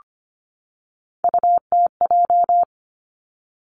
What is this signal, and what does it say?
Morse code, heard as a keyed tone, about 25 wpm, 700 Hz. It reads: UTJ
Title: Three Challenging Letter-Number Combinations 25wpm